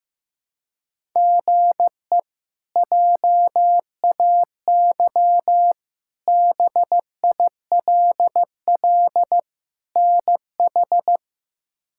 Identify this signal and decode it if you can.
Morse code, heard as a keyed tone, about 16 wpm, 700 Hz. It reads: GE JAY BILL NH
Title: K1USN SST QSO Practice - 15wpm